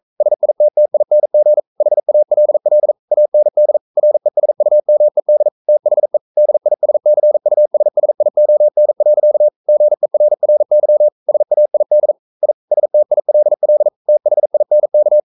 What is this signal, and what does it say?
Morse code, heard as a keyed tone, about 42 wpm, 600 Hz. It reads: SITTING HALL AND RESUMED THE DISCUSSION. GERRY SAID I STILL THINK
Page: Romanen Never af Ken Follet på engelsk.